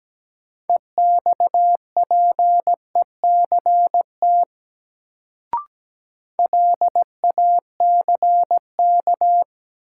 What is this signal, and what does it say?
Morse code, heard as a keyed tone, about 17 wpm, 700 Hz. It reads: EXPECT LACK
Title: Sentences from Top 1100 Words - New Word Drill - 17wpm